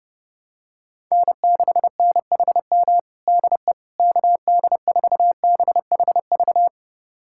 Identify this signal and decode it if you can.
Morse code, heard as a keyed tone, about 30 wpm, 700 Hz. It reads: N6NHM DE KD4BHV
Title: USA Callsign DE Callsign 30wpm